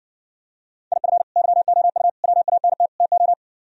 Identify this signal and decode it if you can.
Morse code, heard as a keyed tone, about 60 wpm, 700 Hz. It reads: IF YOU WANT TO